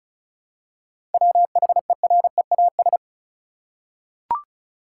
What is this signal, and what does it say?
Morse code, heard as a keyed tone, about 34 wpm, 700 Hz. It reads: WHEREAS
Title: Seven Letter Words - 35wpm